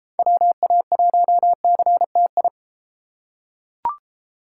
Morse code, heard as a keyed tone, about 32 wpm, 700 Hz. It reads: WA1CTI